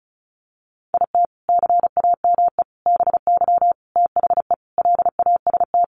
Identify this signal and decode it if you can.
Morse code, heard as a keyed tone, about 34 wpm, 700 Hz. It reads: IT CAME BY THE LAST
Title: Sentences from Top 1000 Words 35wpm